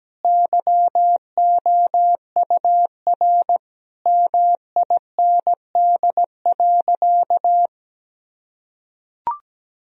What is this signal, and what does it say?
Morse code, heard as a keyed tone, about 17 wpm, 700 Hz. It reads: YOUR MIND.
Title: Sentences from Top 1000 Words - Review Part 3 - 17wpm